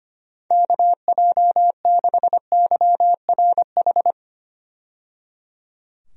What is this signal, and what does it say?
Morse code, heard as a keyed tone, about 25 wpm, 700 Hz. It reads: KJ6YRH
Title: USA Call Signs - 6 Characters - 25wpm